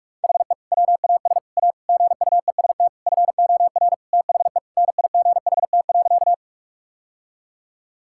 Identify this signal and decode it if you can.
Morse code, heard as a keyed tone, about 46 wpm, 700 Hz. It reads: HE WAS A GUEST FOR THE NIGHT.
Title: Sentences from Top 1600 Words 45wpm